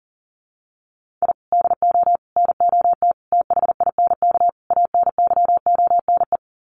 Morse code, heard as a keyed tone, about 40 wpm, 700 Hz. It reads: I DO NOT THINK ANYONE
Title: Sets of 5 Words Encore 40wpm